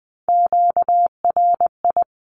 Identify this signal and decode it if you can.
Morse code, heard as a keyed tone, about 20 wpm, 700 Hz. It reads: QRI
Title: Q-Codes 20wpm